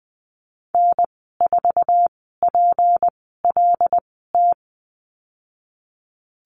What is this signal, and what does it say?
Morse code, heard as a keyed tone, about 19 wpm, spaced as for 14 wpm, 700 Hz. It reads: N4PLT
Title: 213 Lesson 34g - Number 8 - Call Signs 20wpm